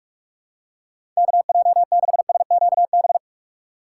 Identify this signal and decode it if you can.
Morse code, heard as a keyed tone, about 47 wpm, 700 Hz. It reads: KJ6SQB
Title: USA Call Signs - 6 Characters - 3x Character Spacing - 45wpm